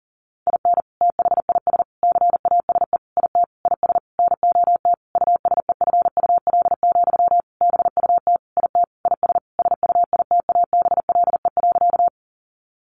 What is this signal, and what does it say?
Morse code, heard as a keyed tone, about 40 wpm, 700 Hz. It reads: IN THIS CASE IT IS NOT USEFUL, BUT IT IS SUITABLE.
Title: Sentences from Top 1400 Words 40wpm